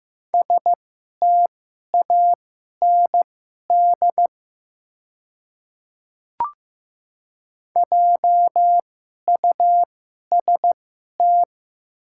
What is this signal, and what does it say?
Morse code, heard as a keyed tone, about 15 wpm, spaced as for 11 wpm, 700 Hz. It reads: STAND JUST
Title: 2x Character Spacing - Top 300 Words - 15wpm